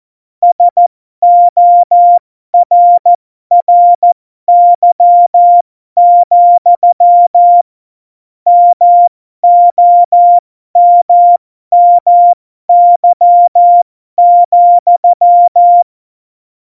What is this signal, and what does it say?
Morse code, heard as a keyed tone, about 14 wpm, spaced as for 12 wpm, 700 Hz. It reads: SORRY, MOMMY,
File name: Morsekurs / Witze